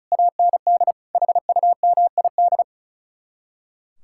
Morse code, heard as a keyed tone, about 45 wpm, 700 Hz. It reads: AND HUMID